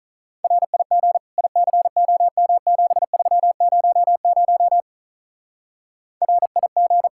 Morse code, heard as a keyed tone, about 45 wpm, 700 Hz. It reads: RIG ICOM7300 RIG